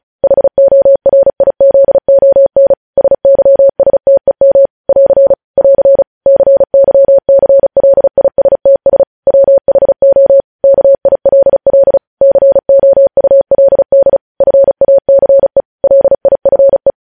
Morse code, heard as a keyed tone, about 36 wpm, 550 Hz. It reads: HORIZON SYSTEM <AR> <AR> CYCLISTS WHO KILL COULD FACE LIFE
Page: News headlines in Morse code at 35 wpm